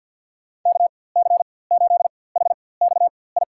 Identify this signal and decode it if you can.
Morse code, heard as a keyed tone, about 50 wpm, spaced as for 23 wpm, 700 Hz. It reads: KC8HXI
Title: USA Call Signs - 4x Character Spacing - 50wpm